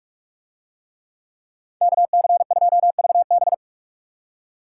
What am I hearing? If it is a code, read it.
Morse code, heard as a keyed tone, about 46 wpm, 700 Hz. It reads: KC2VB